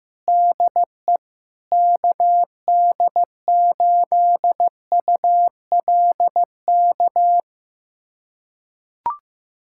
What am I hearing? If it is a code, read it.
Morse code, heard as a keyed tone, about 15 wpm, 700 Hz. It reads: DE KD8ULK
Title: USA Callsign DE Callsign 15wpm